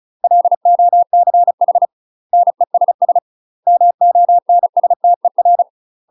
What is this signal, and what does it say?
Morse code, heard as a keyed tone, about 36 wpm, 700 Hz. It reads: LOCH NESS MONSTER
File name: UK MASTER QSOs TRAINING / 35WPM / UK MASTER QSOs 0029